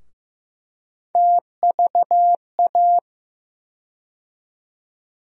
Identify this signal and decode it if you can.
Morse code, heard as a keyed tone, about 15 wpm, 700 Hz. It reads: TVA
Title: USA Call Sign Postfixes - 3 Letters 15wpm